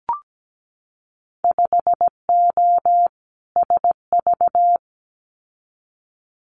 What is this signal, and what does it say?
Morse code, heard as a keyed tone, about 17 wpm, 700 Hz. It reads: 5O SV
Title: Winter Field Day Exchanges Encore 17wpm